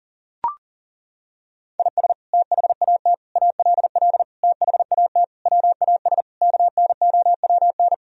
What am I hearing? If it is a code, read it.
Morse code, heard as a keyed tone, about 39 wpm, 700 Hz. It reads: IS THAT ALL THAT WAS KNOWN
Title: Sentences from Top 1000 Words - Review Part 3 - 40wpm